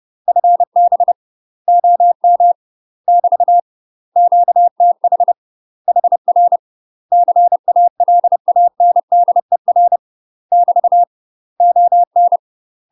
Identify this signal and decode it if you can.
Morse code, heard as a keyed tone, about 30 wpm, 700 Hz. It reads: FB OM <BT> QTH HR CALANDER <BT> ON
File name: UK MASTER QSOs TRAINING / 30WPM / UK MASTER QSOs 0042